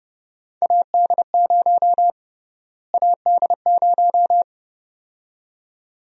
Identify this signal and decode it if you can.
Morse code, heard as a keyed tone, about 32 wpm, 700 Hz. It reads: AD0 AD0
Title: USA Call Sign Prefixes - Repeated 2x - 30wpm